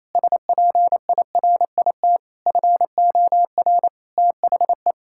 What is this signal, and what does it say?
Morse code, heard as a keyed tone, about 27 wpm, 700 Hz. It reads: SPIRIT FOR THE
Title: Sentences from Top 1600 Words 28wpm